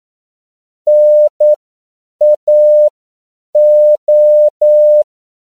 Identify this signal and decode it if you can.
Morse code, heard as a keyed tone, about 9 wpm, spaced as for 7 wpm, 600 Hz. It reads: NAO